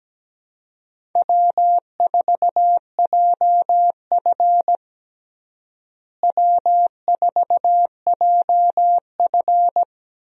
Morse code, heard as a keyed tone, about 17 wpm, 700 Hz. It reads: W4JF W4JF
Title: USA Call Signs - 4 Characters - Repeated 3x - 17wpm